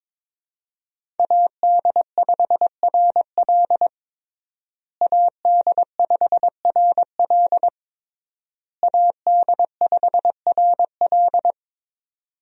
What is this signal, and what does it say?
Morse code, heard as a keyed tone, about 22 wpm, 700 Hz. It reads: AD5RL AD5RL AD5RL
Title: USA Call Signs - 5 Characters - Repeated 3x - 22wpm